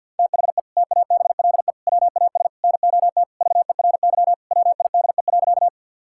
Morse code, heard as a keyed tone, about 52 wpm, 700 Hz. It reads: THE TABLE WAS NOT VERY WIDE.
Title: Sentences from Top 1000 Words - Review Part 6 - 50wpm